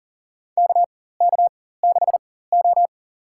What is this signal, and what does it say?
Morse code, heard as a keyed tone, about 40 wpm, spaced as for 19 wpm, 700 Hz. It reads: KK6O
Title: USA Call Signs - 4 Characters - 4x Character Spacing - 40wpm